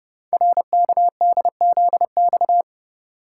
Morse code, heard as a keyed tone, about 30 wpm, 700 Hz. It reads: RKDZX
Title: Code Groups - 5 Letters - 30wpm